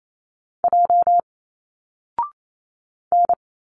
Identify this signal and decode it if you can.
Morse code, heard as a keyed tone, about 28 wpm, 700 Hz. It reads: J N
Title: Single Letters 28wpm